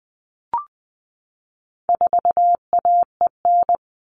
Morse code, heard as a keyed tone, about 19 wpm, 700 Hz. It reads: 4AEN
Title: All Cut Numbers - 4-Digit 20wpm